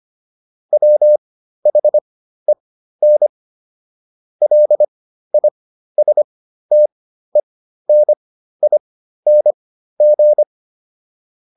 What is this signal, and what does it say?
Morse code, heard as a keyed tone, about 25 wpm, spaced as for 13 wpm, 600 Hz. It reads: WHEN LISTENING